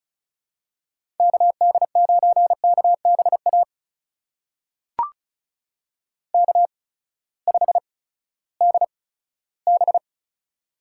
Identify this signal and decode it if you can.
Morse code, heard as a keyed tone, about 36 wpm, spaced as for 18 wpm, 700 Hz. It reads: KD9KBA K5DB
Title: USA Call Signs - 8x Character Spacing - 35wpm